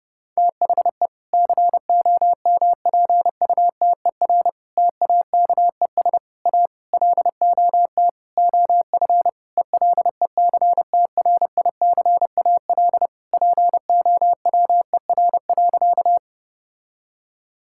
Morse code, heard as a keyed tone, about 30 wpm, 700 Hz. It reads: THE COMPUTER TAKES A LOT OF ELECTRICAL POWER.
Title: Sentences from Top 1000 Words - Review Part 6 - 30wpm